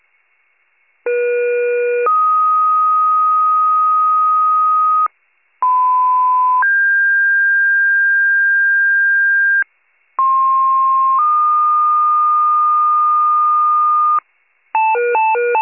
Two-tone page, heard as a one-second tone, then a three-second tone, about 480 then 1250 Hz, about 1010 then 1600 Hz, about 1060 then 1220 Hz. It is heard from 1.1 to 14.2 s.